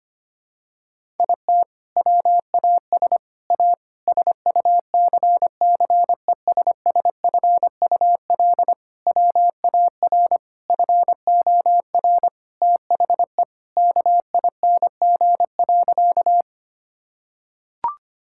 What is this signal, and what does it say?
Morse code, heard as a keyed tone, about 25 wpm, 700 Hz. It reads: IT WAS A SUCCESSFUL WAR FOR THE KING.
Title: Sentences from Top 1000 Words - Review Part 6 - 25wpm